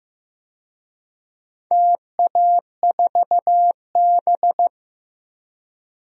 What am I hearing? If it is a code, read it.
Morse code, heard as a keyed tone, about 16 wpm, 700 Hz. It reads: TA4B
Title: DXCC-100 Call Signs - 4 Characters - Repeated 2x - 15wpm